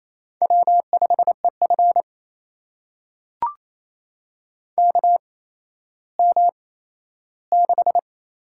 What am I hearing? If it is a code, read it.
Morse code, heard as a keyed tone, about 27 wpm, spaced as for 13 wpm, 700 Hz. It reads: W5EF KM6
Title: USA Call Signs - 4 Characters - 8x Character Spacing - 28wpm